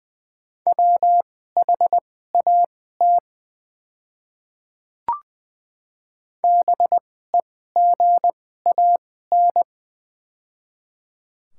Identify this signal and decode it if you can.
Morse code, heard as a keyed tone, about 21 wpm, spaced as for 15 wpm, 700 Hz. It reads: WHAT BEGAN
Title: 2x Character Spacing - Top 300 Words - 20wpm